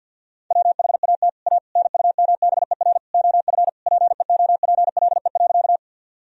Morse code, heard as a keyed tone, about 51 wpm, 700 Hz. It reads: WHAT A NUMBER OF PEOPLE.
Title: Sentences from Top 1000 Words - Review Part 1 - 50wpm